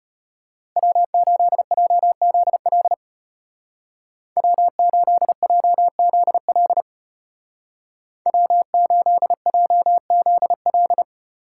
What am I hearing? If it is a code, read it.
Morse code, heard as a keyed tone, about 33 wpm, 700 Hz. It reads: W8JZL W8JZL W8JZL